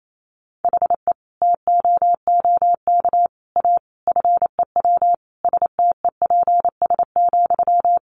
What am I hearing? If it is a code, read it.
Morse code, heard as a keyed tone, about 28 wpm, 700 Hz. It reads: HE TOOK A FEW STEPS,
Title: A Sound Of Thunder 28wpm